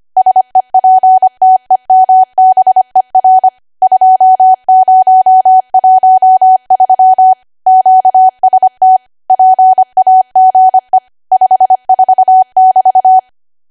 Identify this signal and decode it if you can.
Morse code, heard as a keyed tone, about 25 wpm, 750 Hz. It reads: SEPTEMBER 2013 QST PAGE 54<BT>
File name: cwa / W1AW-Level3 / 25WPM